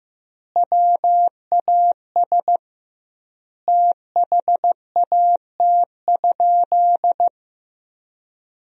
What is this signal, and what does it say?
Morse code, heard as a keyed tone, about 16 wpm, 700 Hz. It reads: WAS THAT?
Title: 2x Word-Spacing - Sentences from Top 100 Words - 15wpm